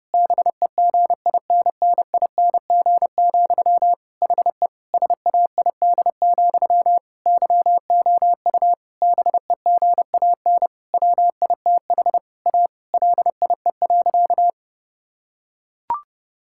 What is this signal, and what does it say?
Morse code, heard as a keyed tone, about 30 wpm, 700 Hz. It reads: BEGINNING, HE SAID, YOU BEGAN WITH A LIE.
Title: Sentences from Top 1600 Words 30wpm